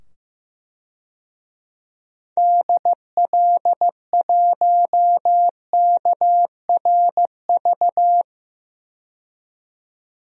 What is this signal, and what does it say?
Morse code, heard as a keyed tone, about 15 wpm, 700 Hz. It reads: DL1KRV
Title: DXCC-100 Call Signs - 6 Characters - 15wpm